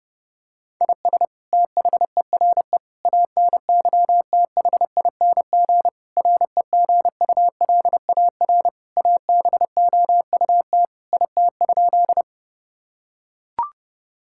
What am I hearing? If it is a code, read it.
Morse code, heard as a keyed tone, about 30 wpm, 700 Hz. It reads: IS THERE ANYTHING REGULAR ABOUT IT?
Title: Sentences from Top 1000 Words - Review Part 6 - 30wpm